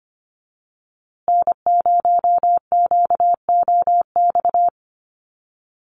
Morse code, heard as a keyed tone, about 25 wpm, 700 Hz. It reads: N0QOX
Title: USA Call Signs Repeated 3x 25wpm